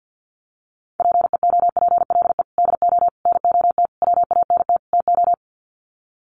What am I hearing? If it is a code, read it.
Morse code, heard as a keyed tone, about 48 wpm, 700 Hz. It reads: PEOPLE DO NOT WANT TO
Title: Sets of 5 Words Encore2 50wpm